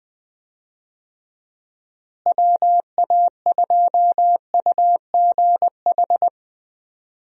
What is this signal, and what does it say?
Morse code, heard as a keyed tone, about 20 wpm, 700 Hz. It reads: WA2UGH
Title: USA Call Signs - 6 Characters - Repeated 2x - 20wpm